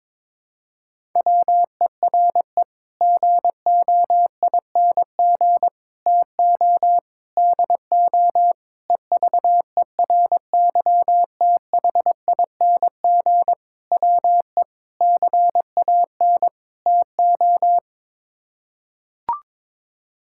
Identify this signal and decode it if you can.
Morse code, heard as a keyed tone, about 21 wpm, 700 Hz. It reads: WERE GOING TO DO EVERYTHING WE CAN TO
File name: practice / 22wpm / Sets of 8 Words - 22wpm